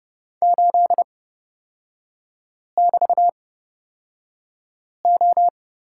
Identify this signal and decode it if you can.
Morse code, heard as a keyed tone, about 30 wpm, spaced as for 5 wpm, 700 Hz. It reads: 8<BT>O
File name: Lesson 36e - BT - Three Characters 30wpm